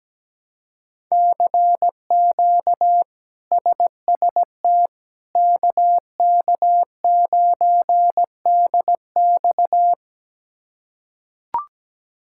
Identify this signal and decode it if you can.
Morse code, heard as a keyed tone, about 17 wpm, 700 Hz. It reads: CQ SST KK9DX
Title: K1USN SST QSO Practice - 17wpm